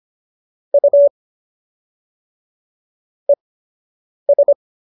Morse code, heard as a keyed tone, about 25 wpm, spaced as for 8 wpm, 570 Hz. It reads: U ES